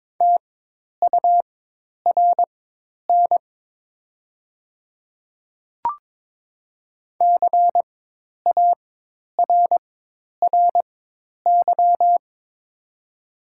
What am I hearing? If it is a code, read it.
Morse code, heard as a keyed tone, about 22 wpm, spaced as for 10 wpm, 690 Hz. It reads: TURN CARRY
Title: 4x Character Spacing - Top 300 Words - 22wpm